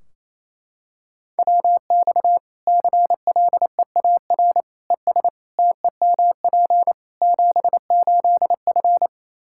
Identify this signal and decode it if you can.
Morse code, heard as a keyed tone, about 28 wpm, 700 Hz. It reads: WX CLEAR ES TEMP 78F
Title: Practice QSO - Exchange 2 - Info - 28wpm